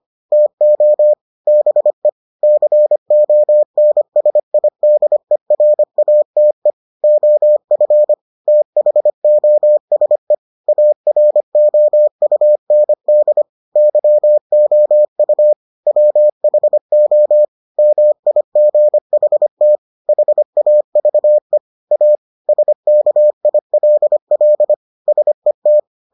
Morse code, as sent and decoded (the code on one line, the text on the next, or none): - --- / -... / -.-. --- -. ... .. -.. .-. .- - . / --- ..-. / - .... --- ... / .- .-. --- ..- -. -.. / -.-- --- ..- / .-- .... --- / -- .. --. .... - / .... .- ...- . / .- / ... -.- .. .-.. .-.. / ... -
TO BE CONSIDERATE OF THOSE AROUND YOU WHO MIGHT HAVE A SKILL SET